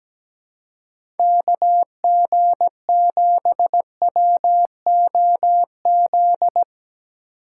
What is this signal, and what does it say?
Morse code, heard as a keyed tone, about 17 wpm, 700 Hz. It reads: KG7WOZ
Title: USA Call Signs - 6 Characters - 4x Character Spacing - 17wpm